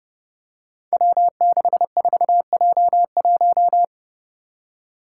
Morse code, heard as a keyed tone, about 30 wpm, 700 Hz. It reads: W64J1
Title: Five Challenging Letter-Number Combinations 30wpm